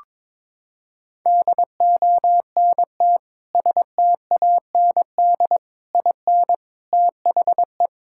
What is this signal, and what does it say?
Morse code, heard as a keyed tone, about 22 wpm, 700 Hz. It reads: DONT STAND IN THE
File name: practice / 22wpm / Sentences with 3-Word Phrases - 22wpm